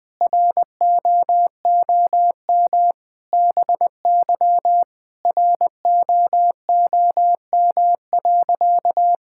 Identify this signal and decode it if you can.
Morse code, heard as a keyed tone, about 20 wpm, 700 Hz. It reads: ROOM BY ROOM.
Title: Sentences from Top 1000 Words - Review Part 5 - 20wpm